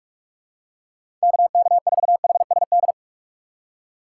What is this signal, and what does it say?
Morse code, heard as a keyed tone, about 46 wpm, 700 Hz. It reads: KK4HSD
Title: USA Call Signs - 4x Character Spacing - 45wpm